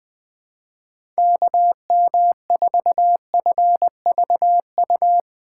Morse code, heard as a keyed tone, about 20 wpm, 700 Hz. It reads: KM4FVU
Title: USA Call Signs - 6 Characters - 2x Character Spacing - 20wpm